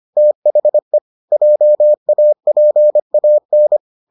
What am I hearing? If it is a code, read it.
Morse code, heard as a keyed tone, about 25 wpm, 600 Hz. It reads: THE JAPAN